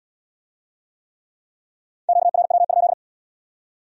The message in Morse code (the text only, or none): QRW?